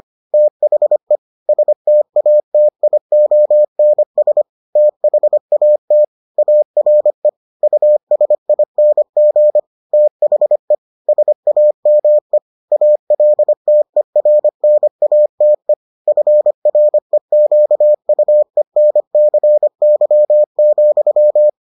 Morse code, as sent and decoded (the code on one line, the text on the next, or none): - .... / ... - .- - .. --- -. ... / - .... .- - / .- .-. / ..- ... .. -. --. / - .... / ... .- -- . / .- .-.. - . .-. -. .- - . / ..-. .-. --.- ..- . -. -.-. -.-- --..--
THE STATIONS THAT ARE USING THE SAME ALTERNATE FREQUENCY,